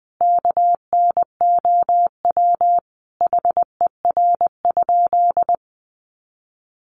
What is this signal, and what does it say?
Morse code, heard as a keyed tone, about 20 wpm, 700 Hz. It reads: KNOW HER?
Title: Sentences from Top 100 Words 20wpm